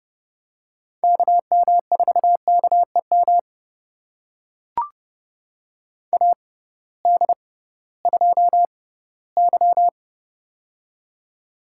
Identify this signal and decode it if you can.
Morse code, heard as a keyed tone, about 30 wpm, spaced as for 17 wpm, 700 Hz. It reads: KM4KEM AD2Y
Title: USA Call Signs - 6x Character Spacing - 30wpm